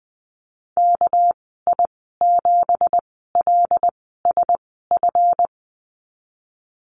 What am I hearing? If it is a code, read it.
Morse code, heard as a keyed tone, about 19 wpm, spaced as for 14 wpm, 700 Hz. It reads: KI7LSF
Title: 200 Lesson 32g - Letter k - Call Signs 20wpm